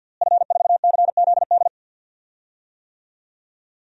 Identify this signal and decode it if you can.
Morse code, heard as a keyed tone, about 51 wpm, 700 Hz. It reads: F4CZD
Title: DXCC-100 Call Signs - 5 Characters - Repeated 2x - 50wpm